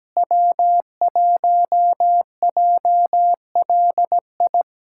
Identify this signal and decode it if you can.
Morse code, heard as a keyed tone, about 17 wpm, 700 Hz. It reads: W1JLI
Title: USA Callsign DE Callsign - 5 Characters - 17wpm